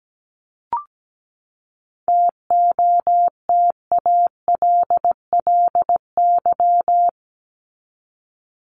Morse code, heard as a keyed tone, about 17 wpm, 700 Hz. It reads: TOTALLY
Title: LY Suffix - Top 100 Words with Base Words 17wpm